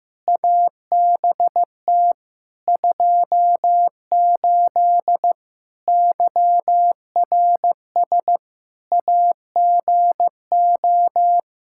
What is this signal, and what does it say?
Morse code, heard as a keyed tone, about 16 wpm, 700 Hz. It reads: ABT 28 YRS AGO
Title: Practice QSO - Exchange 3 - More Info - 15wpm